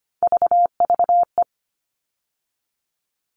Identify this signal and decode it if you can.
Morse code, heard as a keyed tone, about 25 wpm, 700 Hz. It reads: VVE
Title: All Cut Numbers - 3-Digit 25wpm